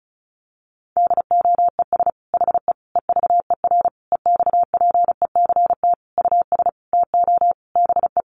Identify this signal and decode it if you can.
Morse code, heard as a keyed tone, about 34 wpm, 700 Hz. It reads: DOES HE EVER EXPECT US TO BE
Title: Sentences from Top 1100 Words 35wpm